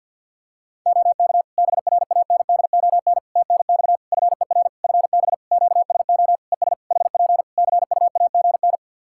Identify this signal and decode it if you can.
Morse code, heard as a keyed tone, about 53 wpm, 700 Hz. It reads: OK BRANDON TNX FER FB QSO ES HP CUAGN
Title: K1USN SST QSO Practice 50wpm